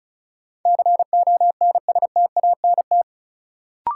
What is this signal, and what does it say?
Morse code, heard as a keyed tone, about 36 wpm, 700 Hz. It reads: CONSTANT
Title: Sentences from Top 1500 Words - New Word Drill - 35wpm